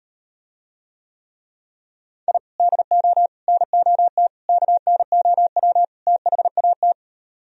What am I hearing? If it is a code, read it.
Morse code, heard as a keyed tone, about 38 wpm, 700 Hz. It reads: I DO NOT KNOW THAT